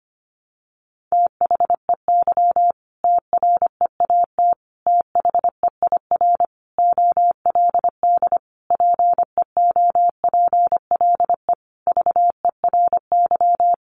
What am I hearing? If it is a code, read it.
Morse code, heard as a keyed tone, about 25 wpm, 700 Hz. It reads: THEY TREAT THEIR OLD PEOPLE VERY
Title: Sentences from Top 1300 Words 25wpm